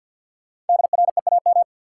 Morse code, heard as a keyed tone, about 52 wpm, 700 Hz. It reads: DREAM
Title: Sentences from Top 1500 Words - New Word Drill Alt - 50wpm